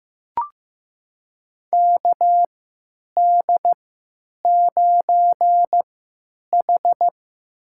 Morse code, heard as a keyed tone, about 15 wpm, spaced as for 9 wpm, 700 Hz. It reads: KD9H
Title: USA Call Signs - 4 Characters - 3x Character Spacing - 15wpm